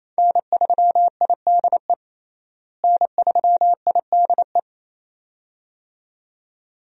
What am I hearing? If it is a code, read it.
Morse code, heard as a keyed tone, about 28 wpm, 700 Hz. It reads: N3IDE N3IDE
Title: USA Call Signs - 5 Characters - Repeated 2x - 28wpm